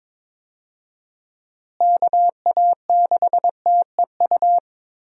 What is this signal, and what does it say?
Morse code, heard as a keyed tone, about 22 wpm, 700 Hz. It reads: KA6TEU
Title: USA Call Signs - 6 Characters - Repeated 2x - 22wpm